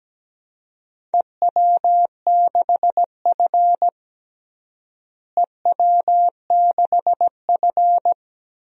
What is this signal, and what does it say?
Morse code, heard as a keyed tone, about 17 wpm, 700 Hz. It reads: EW6F EW6F
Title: DXCC-100 Call Signs - 4 Characters - Repeated 2x - 17wpm